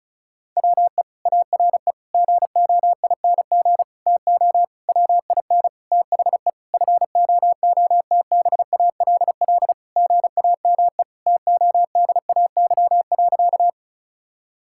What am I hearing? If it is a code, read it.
Morse code, heard as a keyed tone, about 36 wpm, 700 Hz. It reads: WE ARE GOING TO WIN THE FOOTBALL GAME TODAY.
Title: Sentences from Top 1600 Words 35wpm